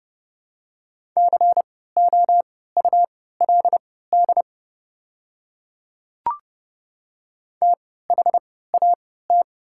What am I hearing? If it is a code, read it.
Morse code, heard as a keyed tone, about 30 wpm, spaced as for 17 wpm, 690 Hz. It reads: COULD THAT
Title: 3x Character Spacing - Top 100 Words - 30wpm